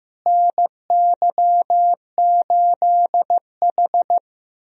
Morse code, heard as a keyed tone, about 16 wpm, 700 Hz. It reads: NY8H
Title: USA Call Signs - 4 Characters - Repeated 3x - 15wpm